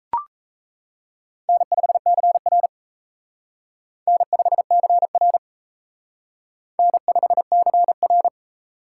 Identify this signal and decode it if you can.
Morse code, heard as a keyed tone, about 34 wpm, 700 Hz. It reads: N5CR N5CR N5CR
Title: Speed Racing - USA 1x2 Call Signs 28wpm